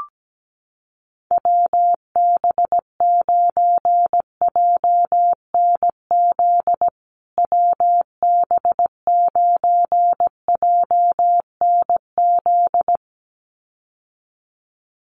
Morse code, heard as a keyed tone, about 17 wpm, 700 Hz. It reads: WB9JNZ WB9JNZ
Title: USA Call Signs Repeated 2x 17wpm